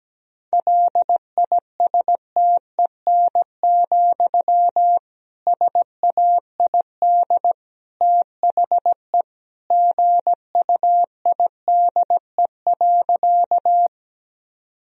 Morse code, heard as a keyed tone, about 17 wpm, 700 Hz. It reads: LISTEN, SAID THE GUIDE.
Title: Sentences from Top 1000 Words - Review Part 3 - 17wpm